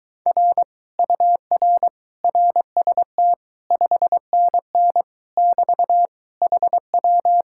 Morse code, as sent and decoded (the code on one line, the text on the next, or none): .-. / ..- .-. / .-. ... - / ..... -. -. / -...- / .... .--
R UR RST 5NN <BT> HW